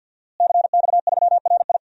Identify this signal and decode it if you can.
Morse code, heard as a keyed tone, about 52 wpm, 700 Hz. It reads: CX3RI
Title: DXCC-100 Call Signs - 5 Characters - Repeated 3x - 50wpm